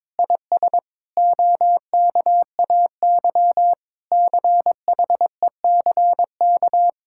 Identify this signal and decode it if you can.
Morse code, heard as a keyed tone, about 22 wpm, 700 Hz. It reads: IS OKAY CHECK
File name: practice / 22wpm / Sentences with 4-Word Phrases - 22wpm